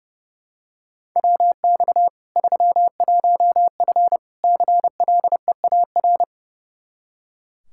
Morse code, heard as a keyed tone, about 32 wpm, 700 Hz. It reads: WX 31F CLEAR
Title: Practice QSO - Exchange 2 - Info - 30wpm